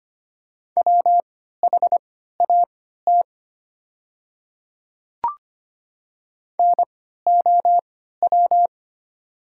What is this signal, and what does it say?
Morse code, heard as a keyed tone, about 25 wpm, spaced as for 14 wpm, 700 Hz. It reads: WHAT NOW
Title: 3x Character Spacing - Top 100 Words - 25wpm